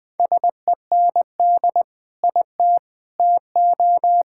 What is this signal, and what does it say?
Morse code, heard as a keyed tone, about 20 wpm, 700 Hz. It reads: SEND IT TO
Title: Sentences with 3-Word Phrases - Spoken Beginning - 20wpm